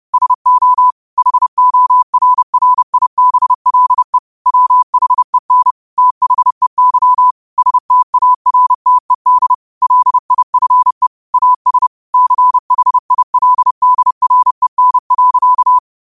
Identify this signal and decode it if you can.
Morse code, heard as a keyed tone, about 30 wpm, 1000 Hz. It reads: SO HORRIBLE WHEN THEY STARTED LIFE AS CHILDREN.